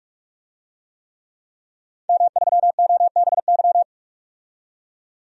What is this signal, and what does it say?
Morse code, heard as a keyed tone, about 48 wpm, 700 Hz. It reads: M3OBY
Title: DXCC-100 Call Signs - 5 Characters - 45wpm